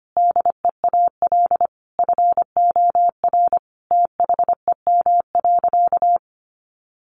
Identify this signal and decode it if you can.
Morse code, heard as a keyed tone, about 25 wpm, 700 Hz. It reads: DEAL FOR THEM.
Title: Sentences from Top 800 Words 25wpm